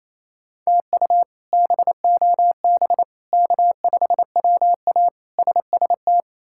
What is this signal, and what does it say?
Morse code, heard as a keyed tone, about 28 wpm, 700 Hz. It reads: TU BOB K5WA SST
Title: K1USN SST QSO Practice - 28wpm